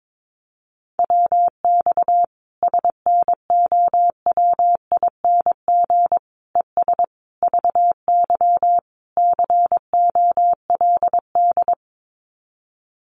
Morse code, heard as a keyed tone, about 22 wpm, 700 Hz. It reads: WX SNOWING ES VY COLD
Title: Common QSO Phrases 22wpm